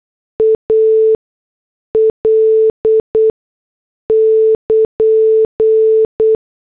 Morse code, heard as a keyed tone, about 8 wpm, spaced as for 6 wpm, 440 Hz. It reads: AL <KN>
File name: morse / headlines